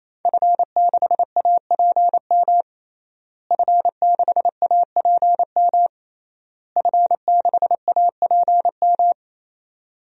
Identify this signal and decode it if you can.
Morse code, heard as a keyed tone, about 27 wpm, 700 Hz. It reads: F6APM F6APM F6APM
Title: DXCC-100 Call Signs - 5 Characters - Repeated 3x - 28wpm